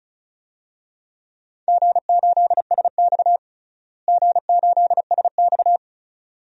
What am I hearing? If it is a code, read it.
Morse code, heard as a keyed tone, about 36 wpm, 700 Hz. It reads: G8SX G8SX
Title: DXCC-100 Call Signs - 4 Characters - Repeated 2x - 35wpm